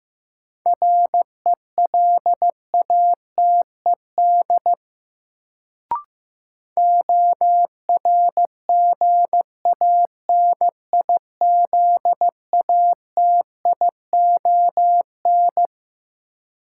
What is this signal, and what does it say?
Morse code, heard as a keyed tone, about 15 wpm, 700 Hz. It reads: RELATED ORGANIZATION
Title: Sentences from Top 1000 Words - Review Part 5 - New Word Drill - 15wpm